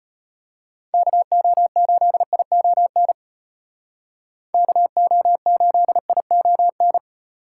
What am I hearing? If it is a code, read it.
Morse code, heard as a keyed tone, about 32 wpm, 700 Hz. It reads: KO8ION KO8ION